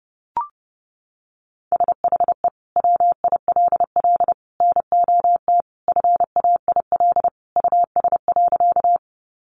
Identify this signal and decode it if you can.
Morse code, heard as a keyed tone, about 30 wpm, 700 Hz. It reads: SHE WILL NOT FAIL US.
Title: Sentences from Top 1100 Words 30wpm